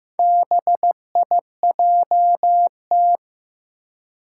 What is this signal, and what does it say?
Morse code, heard as a keyed tone, about 15 wpm, 700 Hz. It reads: BIJT
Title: Four Challenging Letter-Number Combinations 15wpm